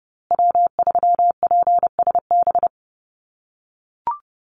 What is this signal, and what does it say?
Morse code, heard as a keyed tone, about 30 wpm, 700 Hz. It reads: W3PSB
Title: USA Call Signs Repeated 2x 30wpm